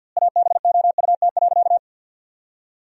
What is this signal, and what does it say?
Morse code, heard as a keyed tone, about 50 wpm, 700 Hz. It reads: ABOUT.
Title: 2x Word-Spacing - Sentences from Top 400 Words - 50wpm